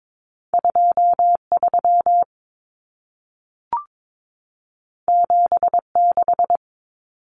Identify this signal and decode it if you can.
Morse code, heard as a keyed tone, about 22 wpm, 700 Hz. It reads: 23 76
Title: Numbers - 2-Digit 22wpm